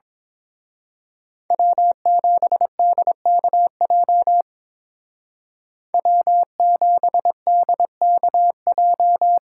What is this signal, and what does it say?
Morse code, heard as a keyed tone, about 26 wpm, 700 Hz. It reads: W7DKJ W7DKJ
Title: Speed Racing - USA 1x3 Call Signs 22wpm